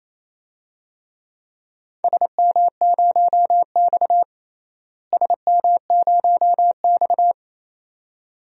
Morse code, heard as a keyed tone, about 28 wpm, 700 Hz. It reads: SM0X SM0X
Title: DXCC-100 Call Signs - 4 Characters - Repeated 2x - 28wpm